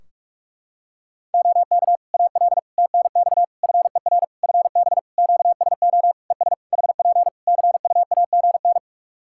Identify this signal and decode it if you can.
Morse code, heard as a keyed tone, about 41 wpm, 700 Hz. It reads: OK AL TNX FER FB QSO ES HP CUAGN
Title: Practice QSO - Exchange 4 - Ending - 45wpm